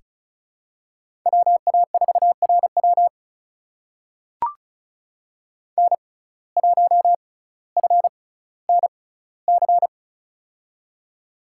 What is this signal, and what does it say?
Morse code, heard as a keyed tone, about 36 wpm, 700 Hz. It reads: W A 4 R W N 1 F N C
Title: USA Call Signs - 5 Characters - 6x Character Spacing - 35wpm